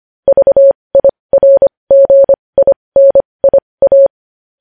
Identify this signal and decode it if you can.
Morse code, heard as a keyed tone, about 25 wpm, spaced as for 20 wpm, 550 Hz. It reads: VIRGINIA
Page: News Headlines in Morse Code at 20 WPM